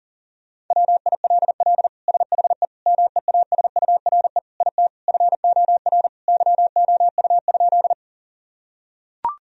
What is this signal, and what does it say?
Morse code, heard as a keyed tone, about 40 wpm, 700 Hz. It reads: WILL SHE MEASURE IT FOR YOU?
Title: Sentences from Top 1000 Words - Review Part 2 - 40wpm